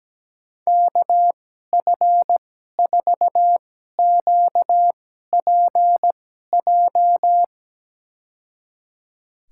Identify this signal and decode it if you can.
Morse code, heard as a keyed tone, about 17 wpm, spaced as for 12 wpm, 700 Hz. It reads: KF4QPJ
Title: USA Call Signs - 2x Character Spacing - 17wpm